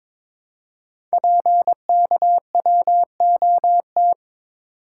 Morse code, heard as a keyed tone, about 22 wpm, 700 Hz. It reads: PKWOT
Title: Code Groups - 5 Letters - 22wpm